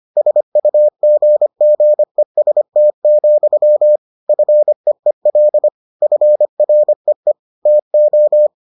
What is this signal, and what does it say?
Morse code, heard as a keyed tone, about 25 wpm, 600 Hz. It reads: SUGGEST, FEEL FREE TO